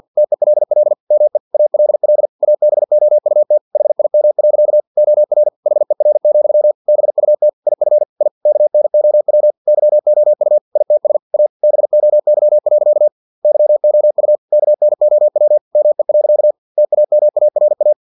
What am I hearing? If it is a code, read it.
Morse code, heard as a keyed tone, about 50 wpm, 600 Hz. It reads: TELL ME ALL ABOUT HIM. OR HER, BUT IF I KNOW YOU ITS A BOY. YOU KNOW ME. TAMARA
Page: Never 49wpm